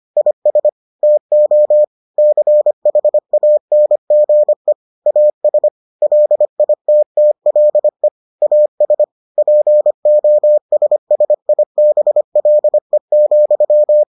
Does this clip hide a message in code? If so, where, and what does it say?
Yes: Morse code from 0.2 to 14.0 s, heard as a keyed tone, about 25 wpm, 600 Hz. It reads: IS TO CHANGE AS LITTLE AS POSSIBLE,